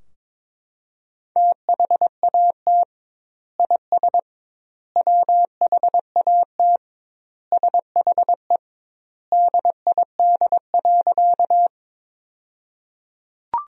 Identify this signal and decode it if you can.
Morse code, heard as a keyed tone, about 21 wpm, 700 Hz. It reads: THAT IS WHAT SHE DID.
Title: 2x Word-Spacing - Sentences from Top 100 Words - 22wpm